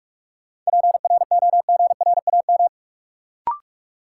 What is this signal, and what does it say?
Morse code, heard as a keyed tone, about 46 wpm, 700 Hz. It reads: PROGRAM
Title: Sentences from Top 1500 Words - New Word Drill Alt - 45wpm